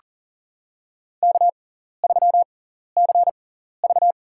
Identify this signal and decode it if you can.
Morse code, heard as a keyed tone, about 40 wpm, spaced as for 14 wpm, 700 Hz. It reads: K3CV
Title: USA Call Signs - 4 Characters - 6x Character Spacing - 40wpm